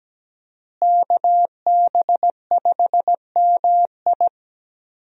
Morse code, heard as a keyed tone, about 17 wpm, 700 Hz. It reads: KB5MI
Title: USA Call Signs - 5 Characters - Repeated 3x - 17wpm